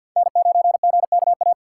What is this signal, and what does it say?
Morse code, heard as a keyed tone, about 53 wpm, 700 Hz. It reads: N9GKA
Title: USA Call Signs - 5 Characters - 4x Character Spacing - 50wpm